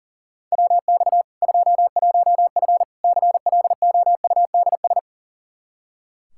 Morse code, heard as a keyed tone, about 40 wpm, 700 Hz. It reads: WX 21F CLOUDS
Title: K1USN SST QSO Practice 40wpm